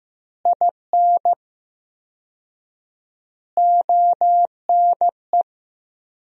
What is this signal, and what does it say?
Morse code, heard as a keyed tone, about 15 wpm, 700 Hz. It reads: IN ONE
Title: 4x Word-Spacing - Sentences from Top 500 Words - 15wpm